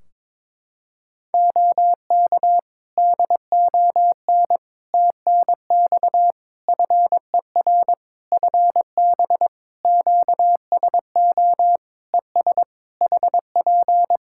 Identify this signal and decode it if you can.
Morse code, heard as a keyed tone, about 22 wpm, 700 Hz. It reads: OK DON TNX FER FB QSO ES HP
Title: Practice QSO - Exchange 4 - Ending - 22wpm